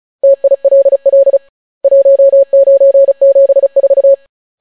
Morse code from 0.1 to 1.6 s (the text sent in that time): TILL